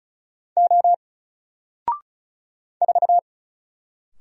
Morse code, heard as a keyed tone, about 35 wpm, 700 Hz. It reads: O 4
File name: Single-Letter-Number-35wpm